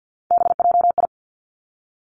Morse code, heard as a keyed tone, about 50 wpm, 700 Hz. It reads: BJI
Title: Three Challenging Letter-Number Combinations 50wpm